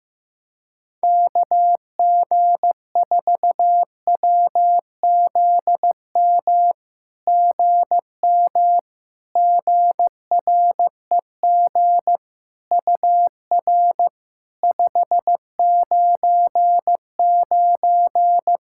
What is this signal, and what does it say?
Morse code, heard as a keyed tone, about 15 wpm, 700 Hz. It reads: KG4WZM GM GREG UR 599
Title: SOTA QSO Practice 15wpm